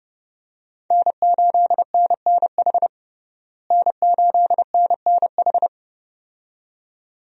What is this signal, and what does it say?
Morse code, heard as a keyed tone, about 32 wpm, 700 Hz. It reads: N8NNH N8NNH
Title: USA Call Signs - 5 Characters - Repeated 2x - 30wpm